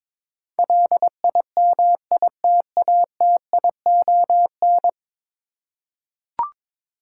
Morse code, heard as a keyed tone, about 22 wpm, 700 Hz. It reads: LIMITATION
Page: morse-code-ninja-practice-22wpm